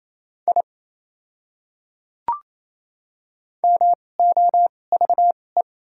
Morse code, heard as a keyed tone, about 28 wpm, spaced as for 20 wpm, 700 Hz. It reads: I MOVE
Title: 2x Character Spacing - Top 300 Words - 28wpm